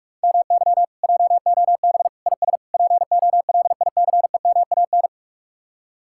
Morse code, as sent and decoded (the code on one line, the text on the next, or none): -- -.-- / .--- --- -... / .. ... / .--. --- .-.. .. -.-. -- .- -.
MY JOB IS POLICEMAN